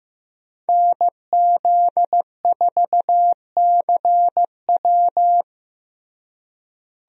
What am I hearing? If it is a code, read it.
Morse code, heard as a keyed tone, about 15 wpm, 700 Hz. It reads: NZ4CW
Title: USA Call Signs - 5 Characters - Repeated 2x - 15wpm